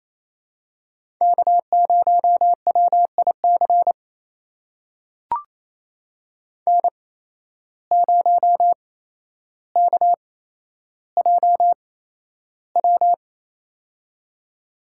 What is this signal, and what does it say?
Morse code, heard as a keyed tone, about 28 wpm, spaced as for 12 wpm, 700 Hz. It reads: K0WIC N0KJW
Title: USA Call Signs - 5 Characters - 8x Character Spacing - 28wpm